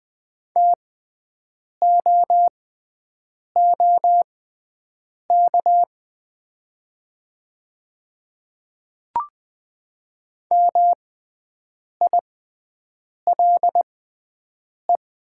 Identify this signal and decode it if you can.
Morse code, heard as a keyed tone, about 20 wpm, spaced as for 7 wpm, 700 Hz. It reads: TOOK MILE
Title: 6x Character Spacing - Top 500 Words - 20wpm